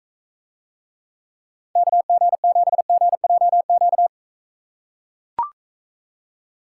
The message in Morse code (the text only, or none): KG7GJQ